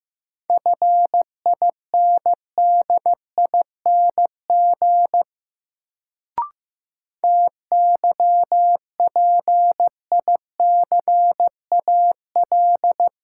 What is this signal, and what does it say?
Morse code, heard as a keyed tone, about 15 wpm, 700 Hz. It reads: FINDING TYPICAL
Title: Sentences from Top 1100 Words - New Word Drill Alt - 15wpm